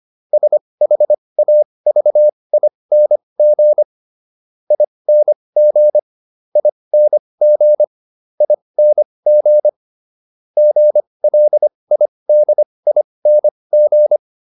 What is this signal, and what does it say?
Morse code, heard as a keyed tone, about 25 wpm, spaced as for 20 wpm, 600 Hz. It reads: SHAVING ING ING ING GLIDING